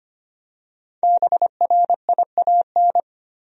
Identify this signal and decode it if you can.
Morse code, heard as a keyed tone, about 25 wpm, 700 Hz. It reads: BRIAN
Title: USA Common Names - 25wpm